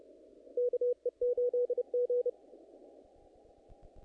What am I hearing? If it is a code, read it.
Morse code, heard as a keyed tone, about 34 wpm, 490 Hz. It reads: KE8G
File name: contests / naqp-cw-k5tr-op-k5pi- -2000